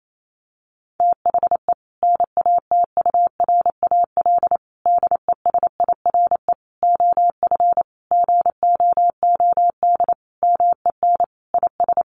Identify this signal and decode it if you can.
Morse code, heard as a keyed tone, about 28 wpm, 700 Hz. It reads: THE NATURAL DESIRE OF GOOD MEN IS
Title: Sentences from Top 1100 Words 28wpm